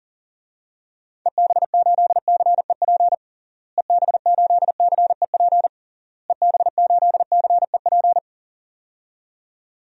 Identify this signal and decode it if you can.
Morse code, heard as a keyed tone, about 40 wpm, 700 Hz. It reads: EB8CEP EB8CEP EB8CEP
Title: DXCC-100 Call Signs - 6 Characters - Repeated 3x - 40wpm